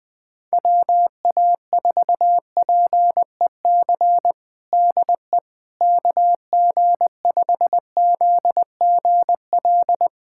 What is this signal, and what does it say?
Morse code, heard as a keyed tone, about 21 wpm, 700 Hz. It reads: WA4PEC DE KG5ZGL
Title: USA Callsign DE Callsign - 6 Characters - 20wpm